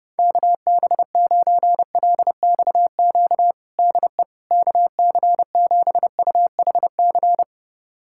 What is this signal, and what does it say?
Morse code, heard as a keyed tone, about 32 wpm, 700 Hz. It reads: KB9LXQ DE KC7UHC
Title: USA Callsign DE Callsign - 6 Characters - 30wpm